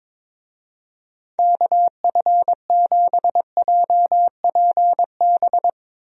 Morse code, heard as a keyed tone, about 22 wpm, 700 Hz. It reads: KF7JPB
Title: USA Call Signs - 6 Characters - 22wpm